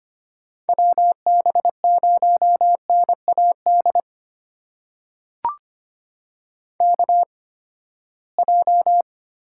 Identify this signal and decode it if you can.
Morse code, heard as a keyed tone, about 25 wpm, spaced as for 17 wpm, 700 Hz. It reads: WB0NAD KJ
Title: USA Call Signs - 6 Characters - 8x Character Spacing - 25wpm